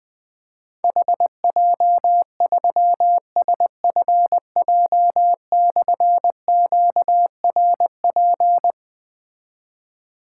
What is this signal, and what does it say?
Morse code, heard as a keyed tone, about 20 wpm, 700 Hz. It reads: HJ3SFJ/QRP
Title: Difficult Call Signs 20wpm